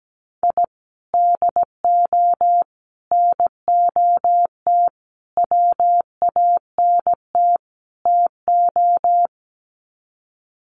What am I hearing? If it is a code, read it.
Morse code, heard as a keyed tone, about 17 wpm, 700 Hz. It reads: I DO NOT WANT TO
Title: Sets of 5 Words 17wpm